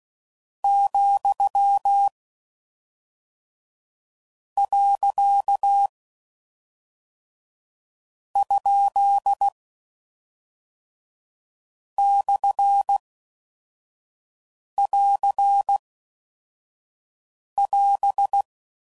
Morse code, heard as a keyed tone, about 16 wpm, 800 Hz. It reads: , . ? / <AR> <AS>